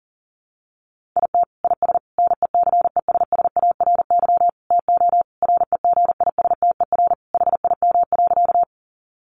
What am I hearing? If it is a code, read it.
Morse code, heard as a keyed tone, about 37 wpm, 700 Hz. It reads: IT IS NECESSARY TO REGISTER HIM.
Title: Sentences from Top 1200 Words 40wpm